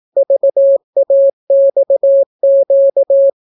Morse code, heard as a keyed tone, about 18 wpm, 550 Hz. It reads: VAXQ